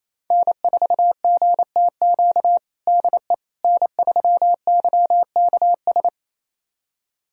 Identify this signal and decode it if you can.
Morse code, heard as a keyed tone, about 28 wpm, 700 Hz. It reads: N4GTQ DE N3YKS
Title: USA Callsign DE Callsign - 5 Characters - 28wpm